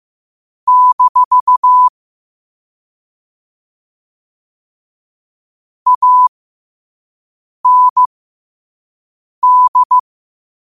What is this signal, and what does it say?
Morse code, heard as a keyed tone, about 15 wpm, spaced as for 5 wpm, 1000 Hz. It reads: - AND
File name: qotdfiles / 5wpm